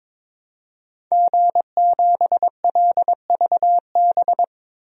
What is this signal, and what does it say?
Morse code, heard as a keyed tone, about 22 wpm, 700 Hz. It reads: G7LVB
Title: DXCC-100 Call Signs - 5 Characters - 22wpm